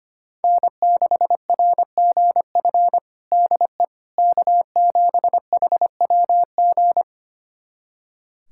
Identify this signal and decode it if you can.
Morse code, heard as a keyed tone, about 25 wpm, 700 Hz. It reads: N6RGF DE K7HWG
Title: USA Callsign DE Callsign - 5 Characters - 25wpm